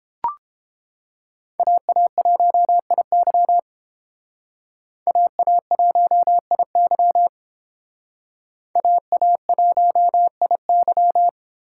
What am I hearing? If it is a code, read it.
Morse code, heard as a keyed tone, about 30 wpm, 700 Hz. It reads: AA1IY AA1IY AA1IY